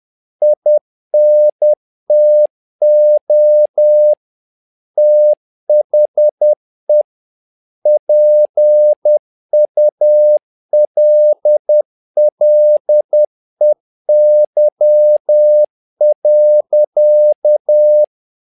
Morse code, heard as a keyed tone, about 10 wpm, 600 Hz. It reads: INTO THE PULLEY.
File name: learning center / tower trouble / tower trouble 10wpm